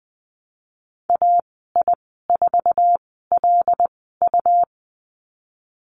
Morse code, heard as a keyed tone, about 20 wpm, spaced as for 15 wpm, 700 Hz. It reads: AI4LU
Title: 122 Lesson 20g - Number 3 - Call Signs 20wpm